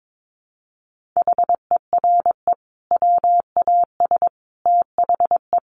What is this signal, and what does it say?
Morse code, heard as a keyed tone, about 22 wpm, 700 Hz. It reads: HERE WAS THE
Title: Sentences from Top 1500 Words 22wpm